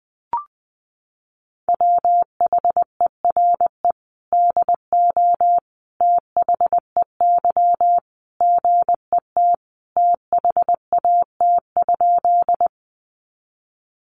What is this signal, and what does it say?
Morse code, heard as a keyed tone, about 20 wpm, 700 Hz. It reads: WHERE DO THEY GET THAT?
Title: Sentences from Top 200 Words 20wpm